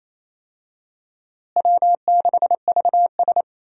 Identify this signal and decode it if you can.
Morse code, heard as a keyed tone, about 28 wpm, 700 Hz. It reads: W6VS
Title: USA Call Signs - 4 Characters - Repeated 3x - 28wpm